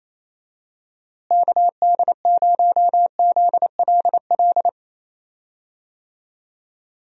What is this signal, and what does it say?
Morse code, heard as a keyed tone, about 28 wpm, 700 Hz. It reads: KD0ZLL
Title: USA Call Signs - 6 Characters - Repeated 2x - 28wpm